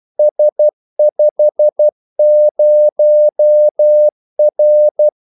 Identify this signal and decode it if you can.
Morse code, heard as a keyed tone, about 12 wpm, 600 Hz. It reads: S50R
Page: Kaldesignaler 12 wpm